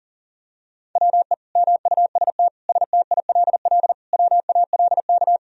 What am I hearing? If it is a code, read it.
Morse code, heard as a keyed tone, about 40 wpm, 700 Hz. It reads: WE MUST STILL WALK
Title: Sentences from Top 1000 Words - Review Part 1 - 40wpm